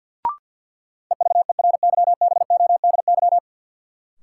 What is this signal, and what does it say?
Morse code, heard as a keyed tone, about 46 wpm, 700 Hz. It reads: EVERYBODY
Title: Sentences from Top 1400 Words - New Word Drill - 50wpm